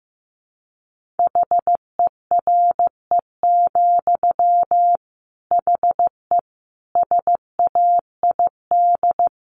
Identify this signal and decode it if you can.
Morse code, heard as a keyed tone, about 15 wpm, 700 Hz. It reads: HERE, HE SAID
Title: Sentences from Top 1500 Words 15wpm